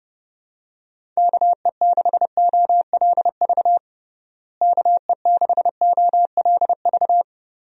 Morse code, heard as a keyed tone, about 30 wpm, 700 Hz. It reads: KE6OLV KE6OLV
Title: USA Call Signs - 6 Characters - Repeated 3x - 30wpm